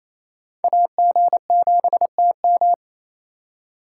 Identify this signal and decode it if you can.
Morse code, heard as a keyed tone, about 34 wpm, 700 Hz. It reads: AG7TM
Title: Speed Racing - USA 2x2 Call Signs 28wpm